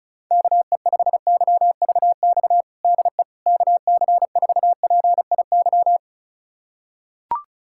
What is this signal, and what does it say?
Morse code, heard as a keyed tone, about 36 wpm, 700 Hz. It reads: KE5YVX DE KC4PIY
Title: USA Callsign DE Callsign - 6 Characters - 35wpm